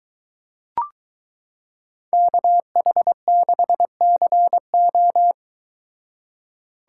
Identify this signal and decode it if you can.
Morse code, heard as a keyed tone, about 22 wpm, 700 Hz. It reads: KH6CO